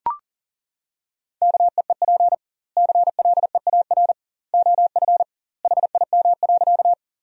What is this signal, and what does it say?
Morse code, heard as a keyed tone, about 40 wpm, 700 Hz. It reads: KEEP CLEAR OF HIM.
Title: 2x Word-Spacing - Sentences from Top 500 Words - 40wpm